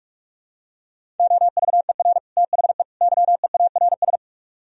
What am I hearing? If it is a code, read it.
Morse code, heard as a keyed tone, about 44 wpm, 700 Hz. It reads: OVER THE YEARS